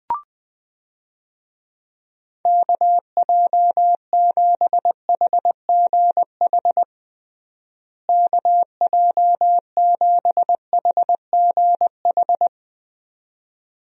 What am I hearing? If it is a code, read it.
Morse code, heard as a keyed tone, about 21 wpm, 700 Hz. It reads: KJ7HGH KJ7HGH
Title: USA Call Signs - 6 Characters - Repeated 2x - 20wpm